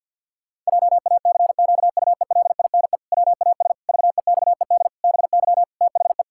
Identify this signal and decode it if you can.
Morse code, heard as a keyed tone, about 48 wpm, 700 Hz. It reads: JACQUELINE WAS VEXED BY THE
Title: Pangram Challenge v2 - 50wpm